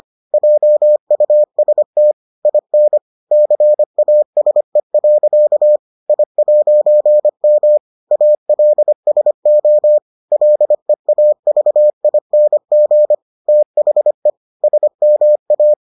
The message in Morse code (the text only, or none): JUST IN CASE. I'M ALSO LEAVING THE SMA